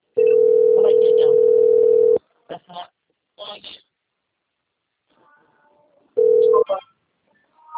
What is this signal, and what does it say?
Ringback tone (North America), 440 + 480 Hz, with speech.
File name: out- - - - - .6733